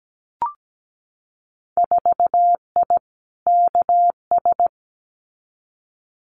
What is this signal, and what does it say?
Morse code, heard as a keyed tone, about 17 wpm, 700 Hz. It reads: 4I KS
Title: Winter Field Day Exchanges Encore 17wpm